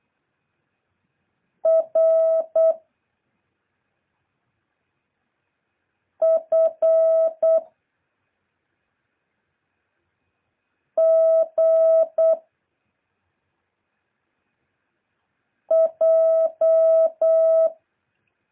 Morse code, heard as a keyed tone, about 8 wpm, 650 Hz. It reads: R F G J
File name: jh8cbh / 2023cw